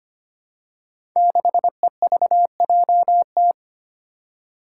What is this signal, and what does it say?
Morse code, heard as a keyed tone, about 25 wpm, 700 Hz. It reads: 6EVJT
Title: Five Challenging Letter-Number Combinations 25wpm